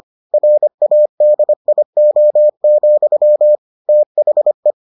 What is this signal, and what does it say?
Morse code, heard as a keyed tone, about 25 wpm, 600 Hz. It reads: RADIO, THE